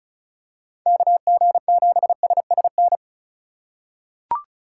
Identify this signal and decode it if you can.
Morse code, heard as a keyed tone, about 34 wpm, 690 Hz. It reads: KG7SSN
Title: USA Call Signs - 6 Characters - 4x Character Spacing - 35wpm